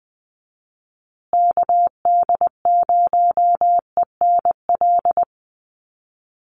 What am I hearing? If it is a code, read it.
Morse code, heard as a keyed tone, about 20 wpm, 700 Hz. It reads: KD0ENL
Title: USA Call Signs Repeated 3x 20wpm